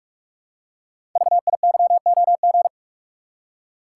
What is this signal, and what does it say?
Morse code, heard as a keyed tone, about 44 wpm, 700 Hz. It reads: VIYOG
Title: Code Groups - 5 Letters - 45wpm